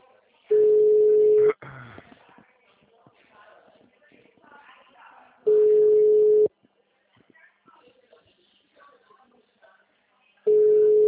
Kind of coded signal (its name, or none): telephone call progress tone